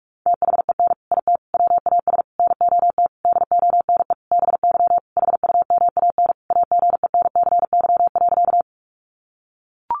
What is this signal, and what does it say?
Morse code, heard as a keyed tone, about 44 wpm, 700 Hz. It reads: THEN IT WAS NOT DONE BY HUMAN AGENCY.
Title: Sentences from Top 1300 Words 45wpm